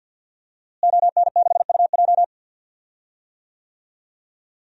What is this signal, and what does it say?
Morse code, heard as a keyed tone, about 46 wpm, 700 Hz. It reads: ON6UJ
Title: DXCC-100 Call Signs - 5 Characters - Repeated 3x - 50wpm